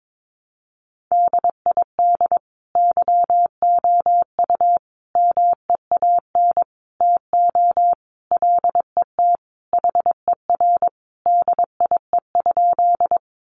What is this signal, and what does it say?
Morse code, heard as a keyed tone, about 22 wpm, 700 Hz. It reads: DID YOU MEAN TO LET HER DIE?
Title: Sentences from Top 1300 Words 22wpm